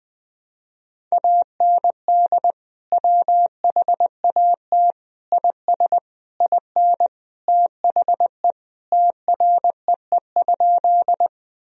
Morse code, heard as a keyed tone, about 20 wpm, 700 Hz. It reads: AND WHAT IS IN THE TREE?
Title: Sentences from Top 1000 Words - Review Part 1 - 20wpm